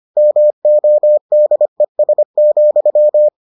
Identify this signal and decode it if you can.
Morse code, heard as a keyed tone, about 25 wpm, 600 Hz. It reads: MODES,